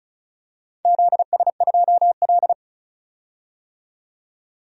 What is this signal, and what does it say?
Morse code, heard as a keyed tone, about 36 wpm, 690 Hz. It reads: ZS2L
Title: DXCC-100 Call Signs - 4 Characters - 35wpm